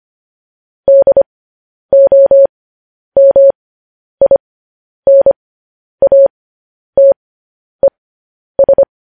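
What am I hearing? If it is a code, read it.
Morse code, heard as a keyed tone, about 25 wpm, spaced as for 10 wpm, 550 Hz. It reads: DOMINATES
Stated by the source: Morse code news headlines practice transmission.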